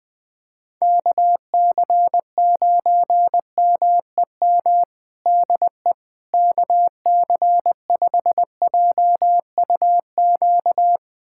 Morse code, heard as a keyed tone, about 20 wpm, 700 Hz. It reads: KC9MEM DE KC5JUQ
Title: USA Callsign DE Callsign - 6 Characters - 20wpm